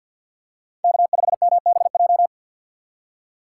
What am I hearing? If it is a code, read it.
Morse code, heard as a keyed tone, about 53 wpm, 700 Hz. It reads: K5MBJ
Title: USA Call Signs - 5 Characters - 6x Character Spacing - 50wpm